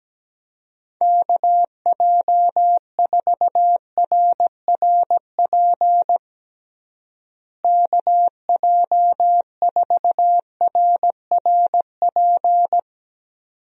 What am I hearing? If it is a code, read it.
Morse code, heard as a keyed tone, about 17 wpm, 700 Hz. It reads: KJ4RRP KJ4RRP
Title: USA Call Signs - 6 Characters - Repeated 2x - 17wpm